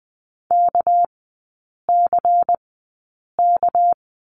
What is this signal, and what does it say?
Morse code, heard as a keyed tone, about 20 wpm, spaced as for 8 wpm, 700 Hz. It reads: KCK
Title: 198 Lesson 32e - Letter k - Three Characters 20wpm